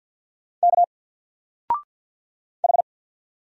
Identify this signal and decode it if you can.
Morse code, heard as a keyed tone, about 48 wpm, 700 Hz. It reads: K H
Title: Single Letter-Number 50wpm